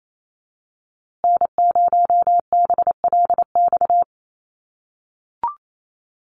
Morse code, heard as a keyed tone, about 28 wpm, 700 Hz. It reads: N0BLX
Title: USA Call Signs Repeated 2x 28wpm